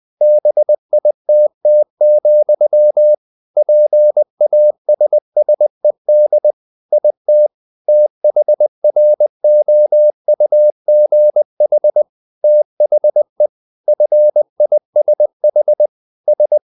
Morse code, heard as a keyed tone, about 20 wpm, 600 Hz. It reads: BITT, PASSED IT THROUGH THE FISH S
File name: the old man and the sea part 2 20wpm